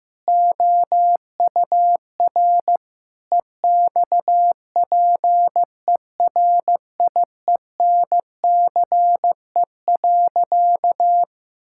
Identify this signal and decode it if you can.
Morse code, heard as a keyed tone, about 15 wpm, 700 Hz. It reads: OUR EXPERIENCE.
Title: Sentences from Top 1000 Words - Review Part 5 - 15wpm